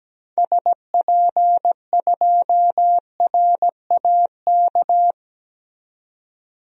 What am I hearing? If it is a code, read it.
Morse code, heard as a keyed tone, about 17 wpm, 700 Hz. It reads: SP2RAK
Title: DXCC-100 Call Signs - 6 Characters - 17wpm